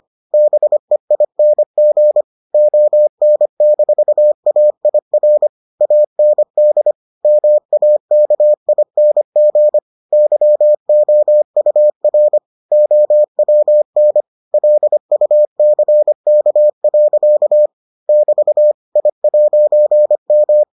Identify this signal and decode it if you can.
Morse code, heard as a keyed tone, about 25 wpm, 600 Hz. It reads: BEING ON-AIR AND MAKING YOUR OWN LUCK. <BT> I'M